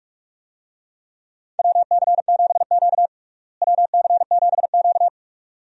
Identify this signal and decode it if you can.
Morse code, heard as a keyed tone, about 46 wpm, 700 Hz. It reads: WC7Q WC7Q
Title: USA Call Signs - 4 Characters - Repeated 2x - 45wpm